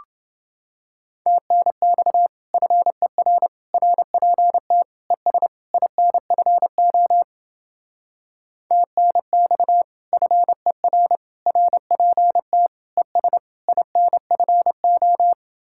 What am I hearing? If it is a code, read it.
Morse code, heard as a keyed tone, about 29 wpm, 700 Hz. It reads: TNX FER RPT ES INFO TNX FER RPT ES INFO